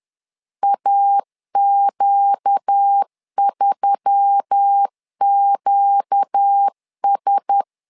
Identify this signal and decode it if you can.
Morse code, heard as a keyed tone, about 10 wpm, 800 Hz. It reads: AQ3QS
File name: LEZIONI / 5 LBVC / 5 LBV 10wpm 2248